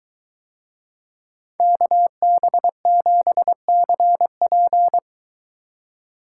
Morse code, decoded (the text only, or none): KB7CP